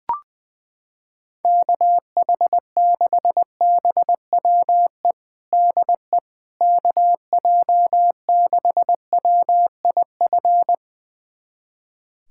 Morse code, heard as a keyed tone, about 20 wpm, 700 Hz. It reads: KH6BWE DE KJ6WIF
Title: USA Callsign DE Callsign 20wpm